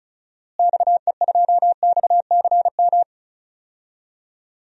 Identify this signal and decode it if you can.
Morse code, heard as a keyed tone, about 34 wpm, 700 Hz. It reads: XE2XCM
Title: DXCC-100 Call Signs - 6 Characters - Repeated 3x - 35wpm